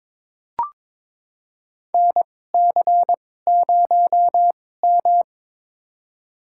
Morse code, heard as a keyed tone, about 22 wpm, spaced as for 16 wpm, 700 Hz. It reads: NC0M
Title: USA Call Signs - 4 Characters - 2x Character Spacing - 22wpm